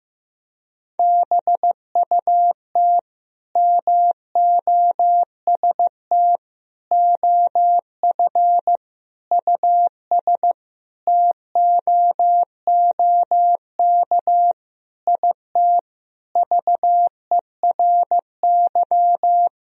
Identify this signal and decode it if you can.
Morse code, heard as a keyed tone, about 15 wpm, 700 Hz. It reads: BUT MOST OF US TOOK IT VERY
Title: Sentences from Top 1000 Words - Review Part 1 - 15wpm